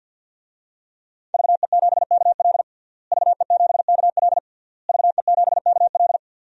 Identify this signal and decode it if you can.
Morse code, heard as a keyed tone, about 52 wpm, 700 Hz. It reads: VE7KL VE7KL VE7KL
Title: DXCC-100 Call Signs - 5 Characters - Repeated 3x - 50wpm